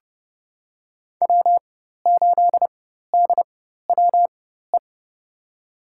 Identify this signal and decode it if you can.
Morse code, heard as a keyed tone, about 30 wpm, spaced as for 14 wpm, 700 Hz. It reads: W8DWE
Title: USA Call Signs - 5 Characters - 4x Character Spacing - 30wpm